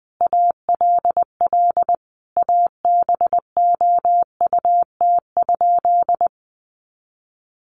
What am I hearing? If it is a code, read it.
Morse code, heard as a keyed tone, about 20 wpm, 700 Hz. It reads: ALL ABOUT?
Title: Sentences from Top 100 Words 20wpm